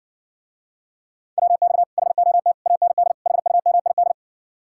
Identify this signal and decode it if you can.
Morse code, heard as a keyed tone, about 60 wpm, 700 Hz. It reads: WX HOT AND HUMID